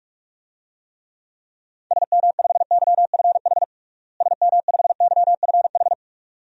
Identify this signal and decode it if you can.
Morse code, heard as a keyed tone, about 46 wpm, 700 Hz. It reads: SM5YFH SM5YFH
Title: DXCC-100 Call Signs - 6 Characters - Repeated 2x - 45wpm